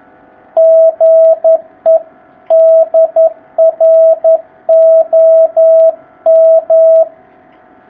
Morse code, heard as a keyed tone, about 11 wpm, 650 Hz, with radio noise. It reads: GE DR OM
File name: jh8cbh / 2019cw